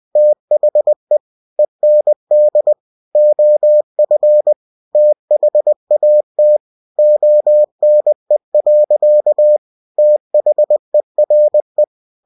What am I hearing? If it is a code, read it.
Morse code, heard as a keyed tone, about 20 wpm, 600 Hz. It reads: THE END OF THAT ONE. THERE
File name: cw-books / winnie-the-pooh / winnie-20wpm-page-0011